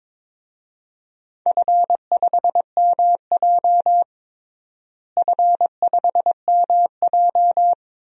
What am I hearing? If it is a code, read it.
Morse code, heard as a keyed tone, about 21 wpm, 700 Hz. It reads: F5MJ F5MJ
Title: DXCC-100 Call Signs - 4 Characters - Repeated 2x - 22wpm